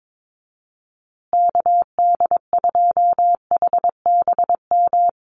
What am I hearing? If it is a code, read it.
Morse code, heard as a keyed tone, about 22 wpm, 700 Hz. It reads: KD2HBM
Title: USA Call Signs Repeated 3x 22wpm